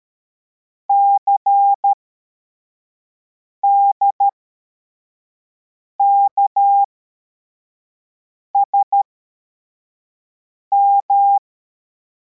Morse code, heard as a keyed tone, about 13 wpm, spaced as for 4 wpm, 800 Hz. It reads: CDKSM